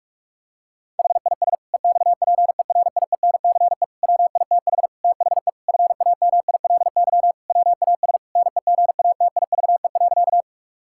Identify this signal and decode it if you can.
Morse code, heard as a keyed tone, about 46 wpm, 700 Hz. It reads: HIS EXPERIENCE WITH THE FAMILY WAS NEGATIVE.
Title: Sentences from Top 1000 Words - Review Part 6 - 45wpm